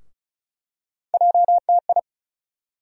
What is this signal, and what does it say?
Morse code, heard as a keyed tone, about 36 wpm, 700 Hz. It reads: JTI
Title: USA Call Sign Postfixes - 3 Letters 35wpm